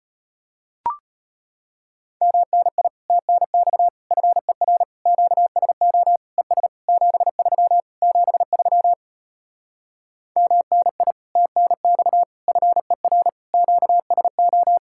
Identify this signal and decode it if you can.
Morse code, heard as a keyed tone, about 33 wpm, 700 Hz. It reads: MNI TNX FER QSO ES 73 73 MNI TNX FER QSO